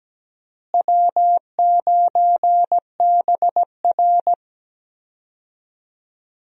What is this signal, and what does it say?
Morse code, heard as a keyed tone, about 17 wpm, 700 Hz. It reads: W9BR
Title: USA Call Signs - 4 Characters - Repeated 3x - 17wpm